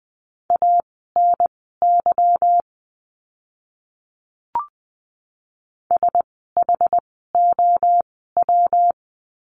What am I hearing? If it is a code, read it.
Morse code, heard as a keyed tone, about 20 wpm, spaced as for 14 wpm, 700 Hz. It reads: ANY SHOW
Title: 230 Lesson 37f - Letter x - Words 20wpm